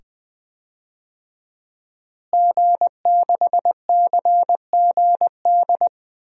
Morse code, heard as a keyed tone, about 20 wpm, 700 Hz. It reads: G6CGD
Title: DXCC-100 Call Signs - 5 Characters - Repeated 2x - 20wpm